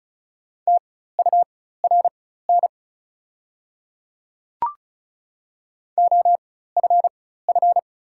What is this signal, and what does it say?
Morse code, heard as a keyed tone, about 34 wpm, spaced as for 16 wpm, 700 Hz. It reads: TURN OFF
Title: 4x Character Spacing - Top 300 Words - 35wpm